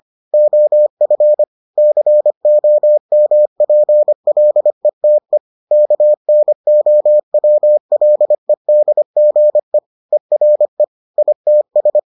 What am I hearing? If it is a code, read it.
Morse code, heard as a keyed tone, about 25 wpm, 600 Hz. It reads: OF COMPLETE KNOWLEDGE ERE ITS